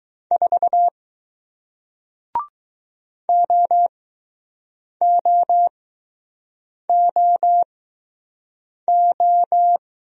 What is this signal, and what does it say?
Morse code, heard as a keyed tone, about 20 wpm, spaced as for 6 wpm, 700 Hz. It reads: 4 OOOO